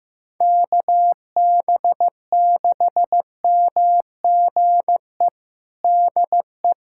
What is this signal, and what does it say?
Morse code, heard as a keyed tone, about 15 wpm, 700 Hz. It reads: KB6MGE DE
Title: USA Callsign DE Callsign 15wpm